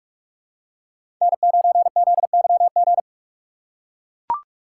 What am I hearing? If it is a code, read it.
Morse code, heard as a keyed tone, about 44 wpm, 700 Hz. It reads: N9ZYG
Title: USA Call Signs - 5 Characters - 6x Character Spacing - 45wpm